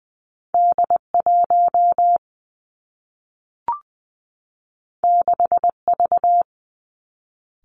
Morse code, heard as a keyed tone, about 19 wpm, 700 Hz. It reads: D 1 6 V
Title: Two Challenging Letter-Number Combinations 20wpm